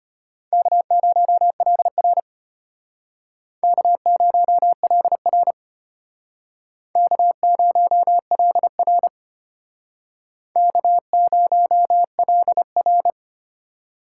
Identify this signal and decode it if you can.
Morse code, heard as a keyed tone, about 31 wpm, 700 Hz. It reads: K0LR K0LR K0LR K0LR